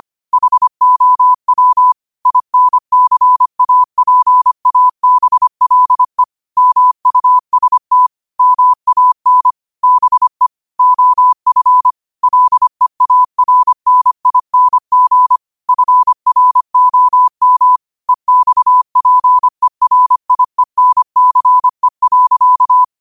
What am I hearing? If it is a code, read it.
Morse code, heard as a keyed tone, about 25 wpm, 1000 Hz. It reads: HOW INCAPABLE MUST MAN BE OF LEARNING FROM EXPERIENCE.
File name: qotdfiles / 25wpm